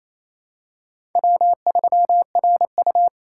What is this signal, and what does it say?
Morse code, heard as a keyed tone, about 28 wpm, 700 Hz. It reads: W3RU
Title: USA Call Signs - 4x Character Spacing - 28wpm